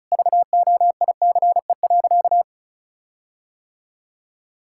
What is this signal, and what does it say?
Morse code, heard as a keyed tone, about 34 wpm, 700 Hz. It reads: VOICE.
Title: Sentences from Top 1600 Words 35wpm